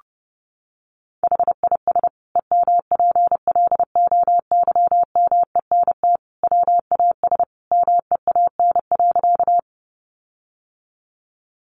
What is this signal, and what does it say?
Morse code, heard as a keyed tone, about 30 wpm, 700 Hz. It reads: HIS EMPLOYMENT WAS MEAN.
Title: Sentences from Top 1500 Words 30wpm